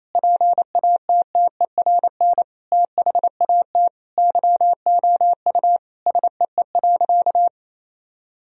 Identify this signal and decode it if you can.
Morse code, heard as a keyed tone, about 28 wpm, 700 Hz. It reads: PATTERN THAT YOU SEE.
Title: Sentences from Top 1000 Words - Review Part 2 - 28wpm